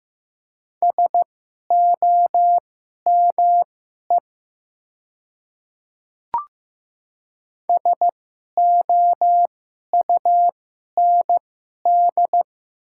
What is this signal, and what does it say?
Morse code, heard as a keyed tone, about 15 wpm, spaced as for 11 wpm, 700 Hz. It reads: SOME SOUND
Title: 2x Character Spacing - Top 100 Words - 15wpm